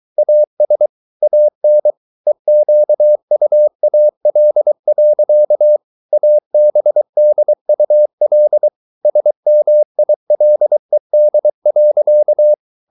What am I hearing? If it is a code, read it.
Morse code, heard as a keyed tone, about 23 wpm, 600 Hz. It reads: AS AN EQUAL. ABDUL SMILED.